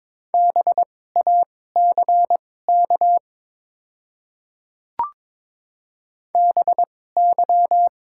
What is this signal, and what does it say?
Morse code, heard as a keyed tone, about 22 wpm, spaced as for 16 wpm, 700 Hz. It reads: BACK BY
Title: 2x Character Spacing - Top 300 Words - 22wpm